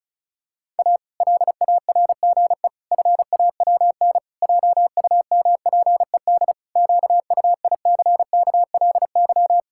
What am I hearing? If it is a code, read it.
Morse code, heard as a keyed tone, about 36 wpm, 700 Hz. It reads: A LARGE FAWN JUMPED QUICKLY
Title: Pangram Challenge v2 - 35wpm